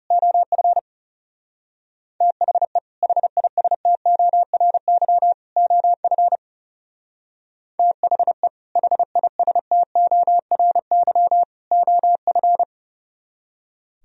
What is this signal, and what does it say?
Morse code, heard as a keyed tone, about 38 wpm, 700 Hz. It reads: OF THE HISTORY OF THE HISTORY OF